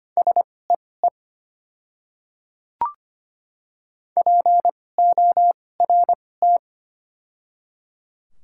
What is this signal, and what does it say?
Morse code, heard as a keyed tone, about 25 wpm, spaced as for 18 wpm, 700 Hz. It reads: SEE PORT
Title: 2x Character Spacing - Top 300 Words - 25wpm